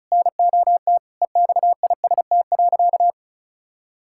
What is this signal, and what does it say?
Morse code, heard as a keyed tone, about 36 wpm, 700 Hz. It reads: NOT EXIST.
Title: Sentences from Top 1000 Words - Review Part 6 - 35wpm